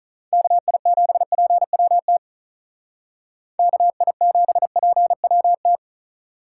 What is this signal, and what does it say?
Morse code, heard as a keyed tone, about 44 wpm, 700 Hz. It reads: KI7PWT KI7PWT